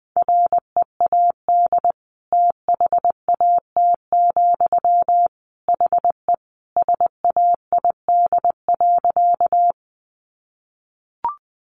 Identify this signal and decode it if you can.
Morse code, heard as a keyed tone, about 20 wpm, 700 Hz. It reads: READ THAT, HE SAID.
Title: Sentences from Top 200 Words 20wpm